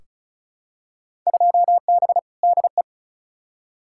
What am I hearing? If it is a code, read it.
Morse code, heard as a keyed tone, about 36 wpm, 700 Hz. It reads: 2B DE
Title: Field Day Exchanges Encore - 35wpm